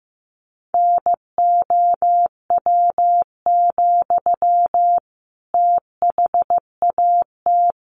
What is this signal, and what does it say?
Morse code, heard as a keyed tone, about 15 wpm, 700 Hz. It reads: NOW, THAT
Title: Sentences from Top 1200 Words 15wpm